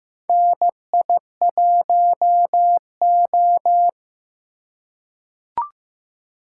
Morse code, heard as a keyed tone, about 15 wpm, 700 Hz. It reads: NI1O
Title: USA Call Signs - 4 Characters - 2x Character Spacing - 15wpm